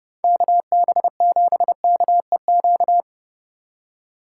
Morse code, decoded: KB7KEQ

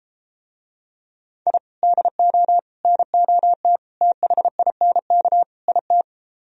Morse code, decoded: I DO NOT THINK IT

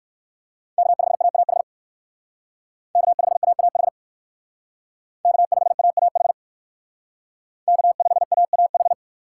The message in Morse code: K5AAH K5AAH K5AAH K5AAH

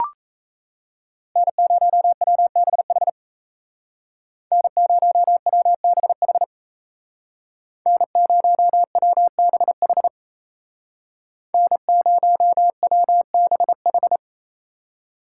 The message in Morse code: N0WBH N0WBH N0WBH N0WBH

